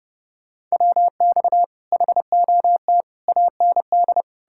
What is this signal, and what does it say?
Morse code, heard as a keyed tone, about 40 wpm, 700 Hz. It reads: WX HOT AND